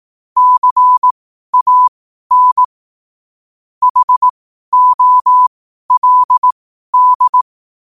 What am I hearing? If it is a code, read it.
Morse code, heard as a keyed tone, about 18 wpm, spaced as for 12 wpm, 1000 Hz. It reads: CAN HOLD
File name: qotdfiles / 12wpm